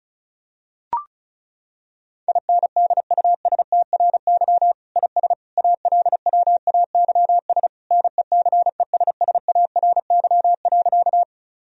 Morse code, heard as a keyed tone, about 35 wpm, 700 Hz. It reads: INDUSTRY IS ALWAYS NECESSARY.
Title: Sentences from Top 1000 Words - Review Part 4 - 35wpm